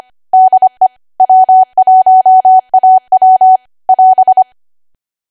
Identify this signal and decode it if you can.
Morse code, heard as a keyed tone, about 25 wpm, 750 Hz. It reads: DE W1AW <AS>